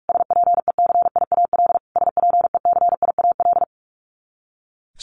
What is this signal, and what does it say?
Morse code, heard as a keyed tone, about 44 wpm, 700 Hz. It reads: SPECIAL SPECIAL
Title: Top 500 Words Repeated 45wpm